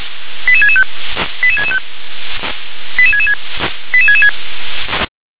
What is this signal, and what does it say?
Five-tone selective call (ZVEI), heard as quick runs of five tones, tones of about 70 ms. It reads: 88665 88665 88666 88666